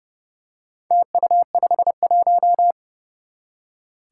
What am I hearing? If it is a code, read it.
Morse code, heard as a keyed tone, about 32 wpm, 700 Hz. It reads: TU51
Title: Four Challenging Letter-Number Combinations 30wpm